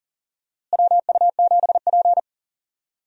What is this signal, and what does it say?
Morse code, heard as a keyed tone, about 40 wpm, 700 Hz. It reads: WU7P
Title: USA Call Signs - 4 Characters - Repeated 2x - 40wpm